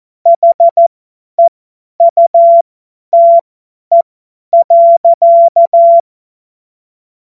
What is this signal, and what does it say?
Morse code, heard as a keyed tone, about 14 wpm, spaced as for 10 wpm, 680 Hz. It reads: HEUTE.
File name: Morsekurs / Witze / Witz 25. April 2023 50 F70